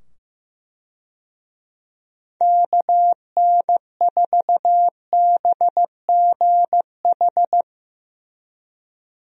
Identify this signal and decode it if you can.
Morse code, heard as a keyed tone, about 15 wpm, 700 Hz. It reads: KN4BGH
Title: USA Call Signs - 6 Characters - 15wpm